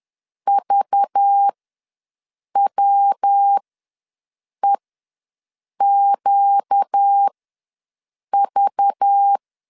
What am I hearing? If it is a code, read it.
Morse code, heard as a keyed tone, about 10 wpm, spaced as for 6 wpm, 800 Hz. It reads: VWEQV
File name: LEZIONI / 5 LBVC / 5 LBV ws 10wpm 3645